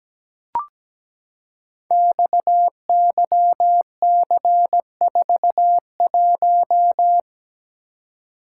Morse code, heard as a keyed tone, about 17 wpm, 690 Hz. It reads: XYC41
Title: Code Groups - 5 Letter:Numbers - 17wpm